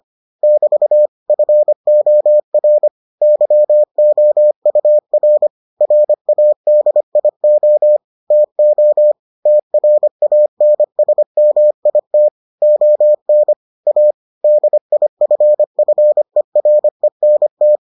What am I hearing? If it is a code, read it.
Morse code, heard as a keyed tone, about 25 wpm, 600 Hz. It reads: <BT> FOR YOUR RADIO TO TRANSMIT ON A DIFFERENT